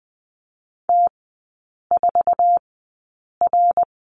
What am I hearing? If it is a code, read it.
Morse code, heard as a keyed tone, about 20 wpm, spaced as for 8 wpm, 700 Hz. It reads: T4R
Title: 053 Lesson 10e - Letter r - Three Characters 20wpm